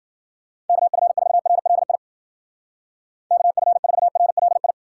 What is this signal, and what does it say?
Morse code, heard as a keyed tone, about 48 wpm, 700 Hz. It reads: KF4RLI KF4RLI
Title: Speed Racing - USA Call Signs 40wpm